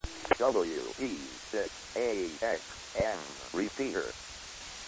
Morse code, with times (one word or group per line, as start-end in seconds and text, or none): none